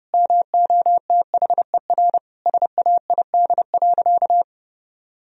Morse code, heard as a keyed tone, about 30 wpm, 700 Hz. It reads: MOTHER SAID.
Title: Sentences from Top 1000 Words - Review Part 4 - 30wpm